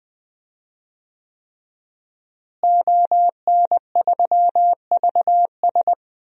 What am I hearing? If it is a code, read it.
Morse code, heard as a keyed tone, about 21 wpm, 700 Hz. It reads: ON3VS
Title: DXCC-100 Call Signs - 5 Characters - Repeated 3x - 20wpm